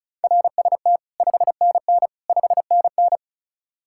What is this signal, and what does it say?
Morse code, heard as a keyed tone, about 46 wpm, 700 Hz. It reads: RST 5NN 5NN